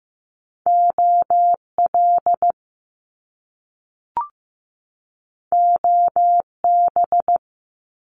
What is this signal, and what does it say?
Morse code, heard as a keyed tone, about 15 wpm, 700 Hz. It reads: OL OB
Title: Top Two Letter Combinations in Words 15wpm